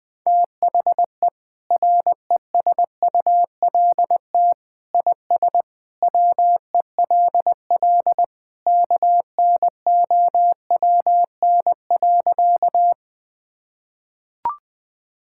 Morse code, heard as a keyed tone, about 20 wpm, 700 Hz. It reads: THE RESULT IS WELL KNOWN.
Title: Sentences from Top 1000 Words - Review Part 5 - 20wpm